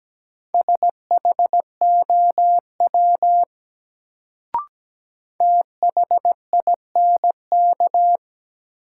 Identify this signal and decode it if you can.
Morse code, heard as a keyed tone, about 17 wpm, 700 Hz. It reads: SHOW THINK
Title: Sentences from Top 200 Words - New Word Drill - 17wpm